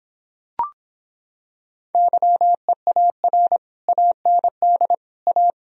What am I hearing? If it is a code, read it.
Morse code, heard as a keyed tone, about 26 wpm, 700 Hz. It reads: YEAR AND A